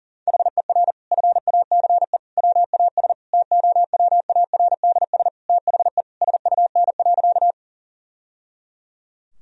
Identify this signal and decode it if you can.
Morse code, heard as a keyed tone, about 40 wpm, 700 Hz. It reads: HER FACE WAS TOWARDS THE SUN.
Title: Sentences from Top 1000 Words - Review Part 2 - 40wpm